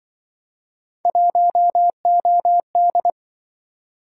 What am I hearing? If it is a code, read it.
Morse code, heard as a keyed tone, about 26 wpm, 700 Hz. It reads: 1OD